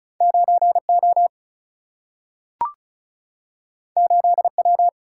Morse code, heard as a keyed tone, about 36 wpm, 700 Hz. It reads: 9 O 8 W
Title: Single-Letter Single-Number 35wpm